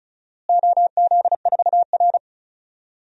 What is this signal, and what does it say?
Morse code, heard as a keyed tone, about 36 wpm, 690 Hz. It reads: OZ4R
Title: DXCC-100 Call Signs - 4 Characters - Repeated 3x - 35wpm